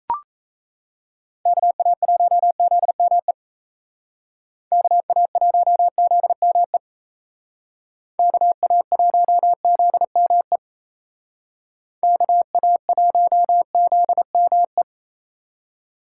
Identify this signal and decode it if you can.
Morse code, heard as a keyed tone, about 37 wpm, 700 Hz. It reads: KA1ZME KA1ZME KA1ZME KA1ZME